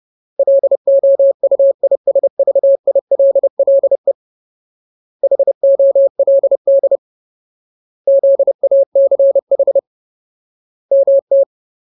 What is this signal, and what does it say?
Morse code, heard as a keyed tone, about 30 wpm, 550 Hz. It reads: LOUISVILLE HOLD ZACH MT
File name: Ultimate Word List / 30WPM Ultimate